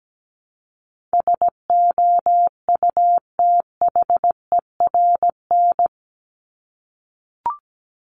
Morse code, heard as a keyed tone, about 17 wpm, 700 Hz. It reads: SOUTHERN
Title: Top 1751-2000 Words 17wpm